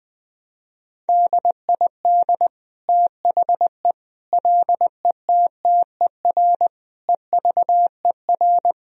Morse code, heard as a keyed tone, about 20 wpm, 700 Hz. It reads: DID THE LETTER EVER
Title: Sentences from Top 1000 Words - Review Part 2 - 20wpm